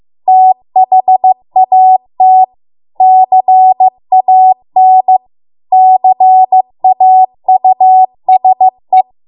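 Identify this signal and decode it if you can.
Morse code, heard as a keyed tone, about 15 wpm, 750 Hz. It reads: THAT CAN CAUSE
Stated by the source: Morse code practice at 15 wpm from the national amateur radio association's code archive.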